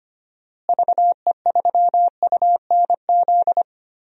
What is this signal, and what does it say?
Morse code, heard as a keyed tone, about 25 wpm, 700 Hz. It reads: VE3UNZ
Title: DXCC-100 Call Signs - 6 Characters - Repeated 2x - 25wpm